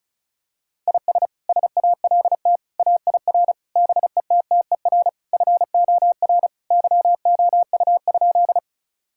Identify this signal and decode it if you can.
Morse code, heard as a keyed tone, about 34 wpm, 700 Hz. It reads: IS SALT AIR BETTER FOR YOU?
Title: Sentences from Top 1000 Words - Review Part 5 - 35wpm